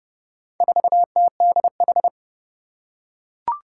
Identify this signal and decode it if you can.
Morse code, heard as a keyed tone, about 30 wpm, 700 Hz. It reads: 4TDH
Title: Four Challenging Letter-Number Combinations 30wpm